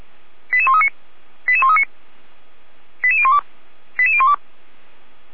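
Five-tone selective call (ZVEI), heard as quick runs of five tones, tones of about 70 ms. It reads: 88139 88139 88012 88012
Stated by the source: ZVEI five-tone call received by SDR, from the Feuerwehr Orlamünde audio server.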